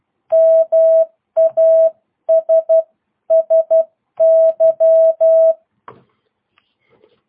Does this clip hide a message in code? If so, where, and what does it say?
Yes: Morse code from 0.3 to 5.5 s, heard as a keyed tone, about 12 wpm, 650 Hz. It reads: MASSY